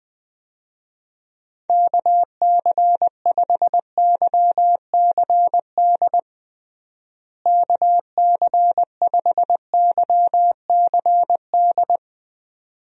Morse code, heard as a keyed tone, about 20 wpm, 700 Hz. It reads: KC5YCD KC5YCD
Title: USA Call Signs - 6 Characters - Repeated 2x - 20wpm